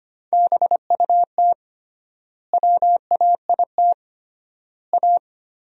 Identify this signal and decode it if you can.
Morse code, heard as a keyed tone, about 25 wpm, 700 Hz. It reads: BUT WAIT A
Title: 3x Word-Spacing - Sentences from Top 500 Words - 25wpm